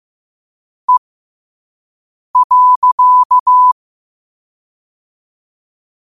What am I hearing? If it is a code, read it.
Morse code, heard as a keyed tone, about 15 wpm, spaced as for 5 wpm, 1000 Hz. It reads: E.
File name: qotdfiles / 5wpm